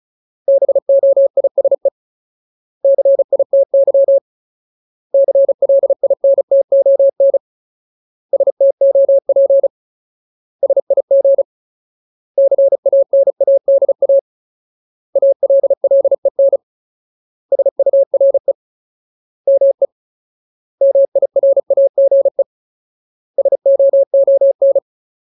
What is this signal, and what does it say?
Morse code, heard as a keyed tone, about 34 wpm, 550 Hz. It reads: BOISE CITY CLINTON STOP SIG CANADA ALLEN SURE ME MIRAGE SOON
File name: Ultimate Word List / 35WPM Ultimate